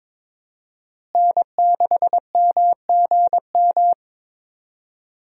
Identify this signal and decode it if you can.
Morse code, heard as a keyed tone, about 22 wpm, 700 Hz. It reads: N6MGM
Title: USA Call Signs - 5 Characters - Repeated 3x - 22wpm